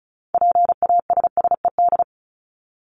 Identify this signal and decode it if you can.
Morse code, heard as a keyed tone, about 34 wpm, 700 Hz. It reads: PASSED
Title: ED Suffix - Top 100 Words with Base Words 35wpm